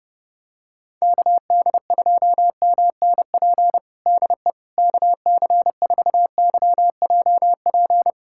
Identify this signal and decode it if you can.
Morse code, heard as a keyed tone, about 32 wpm, 700 Hz. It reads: KD2MNP DE KC4YJP
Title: USA Callsign DE Callsign 30wpm